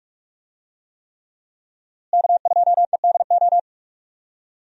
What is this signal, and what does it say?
Morse code, heard as a keyed tone, about 46 wpm, 700 Hz. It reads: K2EDO
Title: USA Call Signs - 5 Characters - Repeated 3x - 45wpm